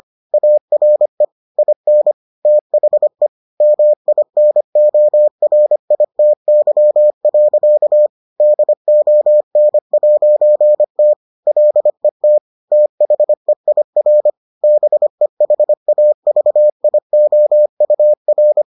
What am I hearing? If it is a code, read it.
Morse code, heard as a keyed tone, about 25 wpm, 600 Hz. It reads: ARE IN THE MINORITY. DON'T LET THEIR BEHAVIOUR